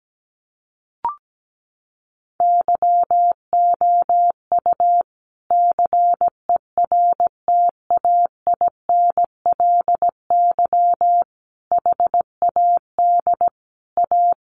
Morse code, heard as a keyed tone, about 17 wpm, 700 Hz. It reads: YOU CERTAINLY HAD A
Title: Sentences from Top 1200 Words 17wpm